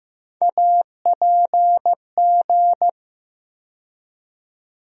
Morse code, heard as a keyed tone, about 15 wpm, 700 Hz. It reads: APG
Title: USA Call Sign Postfixes - 3 Letters 15wpm